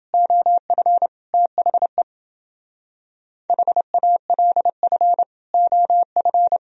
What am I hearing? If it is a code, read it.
Morse code, heard as a keyed tone, about 29 wpm, 700 Hz. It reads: OF THE HALF OF